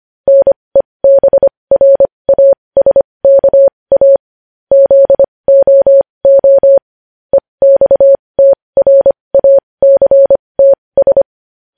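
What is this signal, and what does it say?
Morse code, heard as a keyed tone, about 25 wpm, spaced as for 20 wpm, 550 Hz. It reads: NEBRASKA ZOO EXTRACTS
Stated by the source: Morse practice sending news headlines at 20 wpm.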